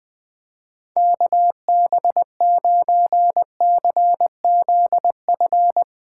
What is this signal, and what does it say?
Morse code, heard as a keyed tone, about 20 wpm, 700 Hz. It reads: KB9CZF
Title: Speed Racing - USA 2x3 Call Signs 20wpm